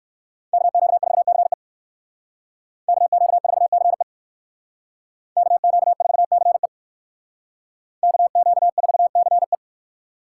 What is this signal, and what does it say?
Morse code, heard as a keyed tone, about 64 wpm, 690 Hz. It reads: KQ4CE KQ4CE KQ4CE KQ4CE